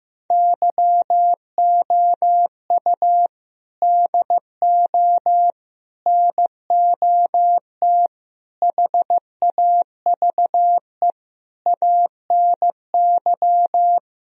Morse code, heard as a keyed tone, about 14 wpm, 700 Hz. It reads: YOU DO NOT HAVE ANY
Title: Sentences from Top 1000 Words - Review Part 4 - 15wpm